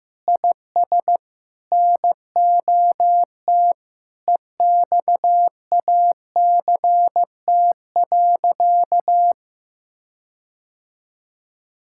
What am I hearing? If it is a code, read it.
Morse code, heard as a keyed tone, about 15 wpm, 700 Hz. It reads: IS NOT EXACT.
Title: Sentences from Top 1600 Words 15wpm